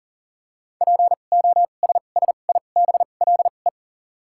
Morse code, heard as a keyed tone, about 37 wpm, spaced as for 28 wpm, 700 Hz. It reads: POSSIBLE
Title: 2x Character Spacing - Top 500 Words - 40wpm